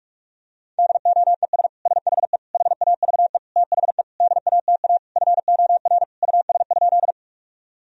Morse code, heard as a keyed tone, about 41 wpm, 700 Hz. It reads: DOES SHE HAVE THE DATA FOR US?
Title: Sentences from Top 1000 Words - Review Part 3 - 45wpm